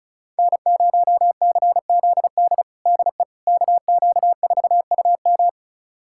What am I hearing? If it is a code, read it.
Morse code, heard as a keyed tone, about 36 wpm, 700 Hz. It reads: N0CZD DE KQ4UM
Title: USA Callsign DE Callsign - 5 Characters - 35wpm